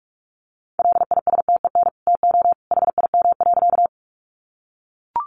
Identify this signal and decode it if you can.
Morse code, heard as a keyed tone, about 44 wpm, 700 Hz. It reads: LISTEN TO HIM.
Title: Sentences from Top 400 Words 45wpm